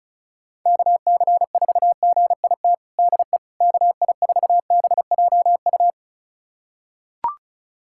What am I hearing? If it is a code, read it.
Morse code, heard as a keyed tone, about 36 wpm, 700 Hz. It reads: KC4GIT DE KI4BJU
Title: USA Callsign DE Callsign 35wpm